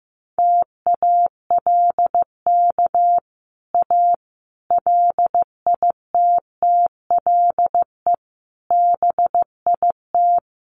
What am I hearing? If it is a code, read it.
Morse code, heard as a keyed tone, about 15 wpm, 700 Hz. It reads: TALK A LITTLE BIT
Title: Sets of 5 Words Encore 15wpm